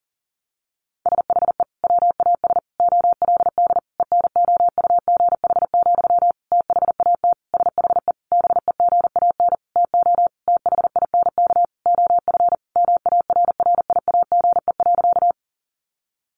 Morse code, heard as a keyed tone, about 40 wpm, 700 Hz. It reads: SHE WAS OLD ENOUGH, THAT SHE BEGAN TO THINK OF MARRIAGE.
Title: Sentences from Top 800 Words 40wpm